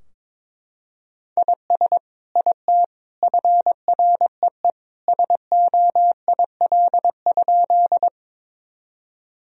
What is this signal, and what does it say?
Morse code, heard as a keyed tone, about 22 wpm, 700 Hz. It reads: IS IT FREE SOIL?
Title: Sentences from Top 1000 Words - Review Part 4 - 22wpm